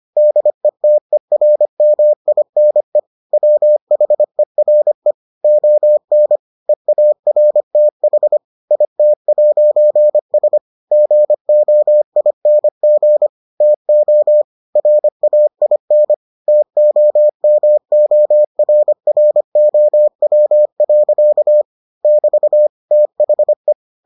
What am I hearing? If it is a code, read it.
Morse code, heard as a keyed tone, about 25 wpm, 600 Hz. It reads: DETERMINE WHERE ON EARTH IT'S GOING TO RAIN TOMORROW. <BT> THE